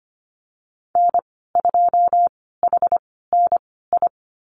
Morse code, heard as a keyed tone, about 25 wpm, spaced as for 16 wpm, 700 Hz. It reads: N2HNI
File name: Lesson 15g - Number 5 - Call Signs 25wpm